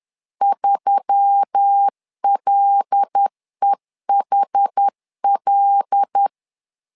Morse code, heard as a keyed tone, about 10 wpm, 800 Hz. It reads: 3LEHL
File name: LEZIONI / 5 LBVC / 5 L 10wpm 2259